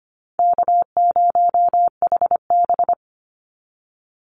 Morse code, heard as a keyed tone, about 25 wpm, 700 Hz. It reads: K0HB
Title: Easy Call Signs - 25wpm